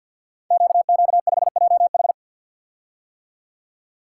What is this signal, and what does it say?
Morse code, heard as a keyed tone, about 52 wpm, 690 Hz. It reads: QQ5JH
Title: Code Groups - 5 Letter:Numbers - 50wpm